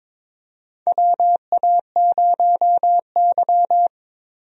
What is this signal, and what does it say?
Morse code, heard as a keyed tone, about 22 wpm, 700 Hz. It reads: WA0Y
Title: USA Call Signs - 4 Characters - Repeated 3x - 22wpm